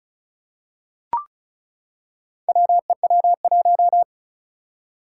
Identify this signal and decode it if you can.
Morse code, heard as a keyed tone, about 36 wpm, 700 Hz. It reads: WEW1
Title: Four Challenging Letter-Number Combinations 35wpm